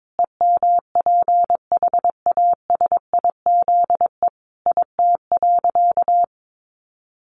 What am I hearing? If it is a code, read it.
Morse code, heard as a keyed tone, about 22 wpm, 700 Hz. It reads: EMPHASIZE IT.
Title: Sentences from Top 1500 Words 22wpm